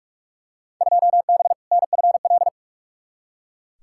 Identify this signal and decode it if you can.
Morse code, heard as a keyed tone, about 42 wpm, 700 Hz. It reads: 2B NFL
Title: Field Day Exchanges Encore - 45wpm